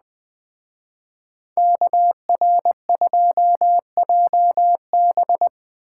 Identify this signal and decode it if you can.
Morse code, heard as a keyed tone, about 20 wpm, 700 Hz. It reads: KR2JB
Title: Speed Racing - USA 2x2 Call Signs 17wpm